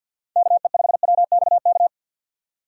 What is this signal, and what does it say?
Morse code, heard as a keyed tone, about 53 wpm, 700 Hz. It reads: KE5WXK
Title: USA Call Signs - 4x Character Spacing - 50wpm